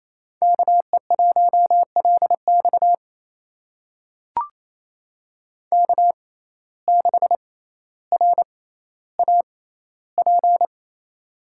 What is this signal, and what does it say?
Morse code, heard as a keyed tone, about 28 wpm, spaced as for 14 wpm, 700 Hz. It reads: KE1LX K6RAP
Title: USA Call Signs - 6x Character Spacing - 28wpm